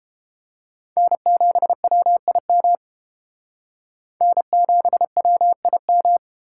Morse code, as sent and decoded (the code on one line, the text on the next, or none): -. --... .-- .. -- / -. --... .-- .. --
N7WIM N7WIM